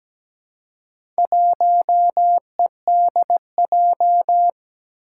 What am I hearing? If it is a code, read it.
Morse code, heard as a keyed tone, about 17 wpm, 700 Hz. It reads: 1EDJ
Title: Four Challenging Letter-Number Combinations 17wpm